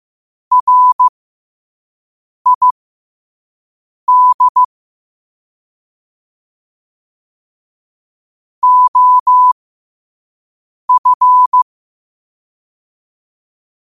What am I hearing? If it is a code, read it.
Morse code, heard as a keyed tone, about 15 wpm, spaced as for 5 wpm, 1000 Hz. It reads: RID OF
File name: qotdfiles / 5wpm